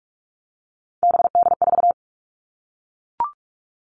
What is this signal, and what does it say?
Morse code, heard as a keyed tone, about 44 wpm, 700 Hz. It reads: BD4
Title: Callsign Prefixes for Top 100 DXCC - 45wpm